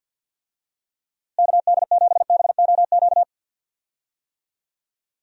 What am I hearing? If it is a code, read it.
Morse code, heard as a keyed tone, about 46 wpm, 700 Hz. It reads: KD7BOQ
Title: USA Call Signs - 6 Characters - Repeated 3x - 50wpm